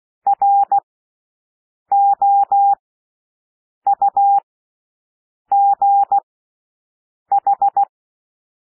Morse code, heard as a keyed tone, about 16 wpm, spaced as for 6 wpm, 800 Hz. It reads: ROUGH
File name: cwmp3 / W1 SOUND WORDS 16 wpm